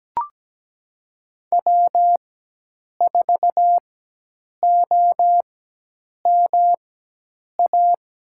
Morse code, heard as a keyed tone, about 17 wpm, spaced as for 8 wpm, 700 Hz. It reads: W4OMA
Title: USA Call Signs - 5 Characters - 4x Character Spacing - 17wpm